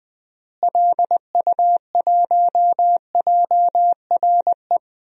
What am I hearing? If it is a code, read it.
Morse code, heard as a keyed tone, about 20 wpm, 700 Hz. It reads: LU1JRE
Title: DXCC-100 Call Signs - 6 Characters - Repeated 3x - 20wpm